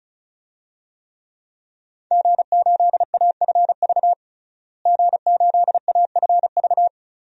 Morse code, heard as a keyed tone, about 34 wpm, 700 Hz. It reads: G8AFV G8AFV
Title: DXCC-100 Call Signs - 5 Characters - Repeated 2x - 35wpm